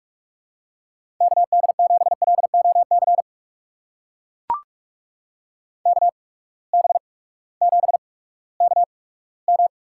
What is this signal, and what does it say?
Morse code, heard as a keyed tone, about 44 wpm, spaced as for 20 wpm, 700 Hz. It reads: KD7LOC KB7KM
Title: USA Call Signs - 6 Characters - 8x Character Spacing - 45wpm